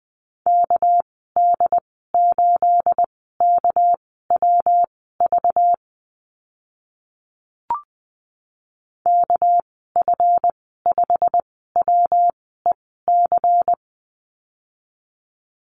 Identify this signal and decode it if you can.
Morse code, heard as a keyed tone, about 20 wpm, spaced as for 14 wpm, 700 Hz. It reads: KD8KWV KF5WEC
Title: 219 Lesson 35g - Number 0 - Call Signs 20wpm